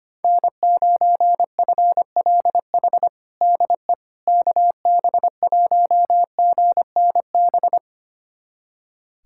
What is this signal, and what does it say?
Morse code, heard as a keyed tone, about 25 wpm, 700 Hz. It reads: N9FLH DE KB1GNB
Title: USA Callsign DE Callsign 25wpm